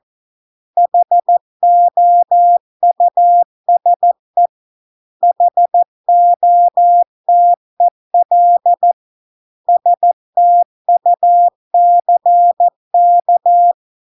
Morse code, heard as a keyed tone, about 14 wpm, 700 Hz. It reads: HOUSE HOTEL STUCK